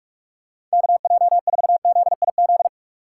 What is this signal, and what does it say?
Morse code, heard as a keyed tone, about 46 wpm, 700 Hz. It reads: KJ4ZIZ